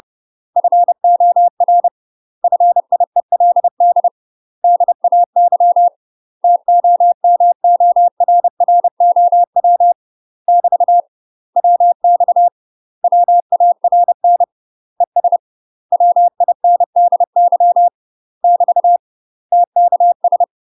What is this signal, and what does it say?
Morse code, heard as a keyed tone, about 30 wpm, 700 Hz. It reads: FOR FIELD DAY TOMORROW <BT> WX WARN ES WINDY <BT> TKS
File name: UK MASTER QSOs TRAINING/30WPM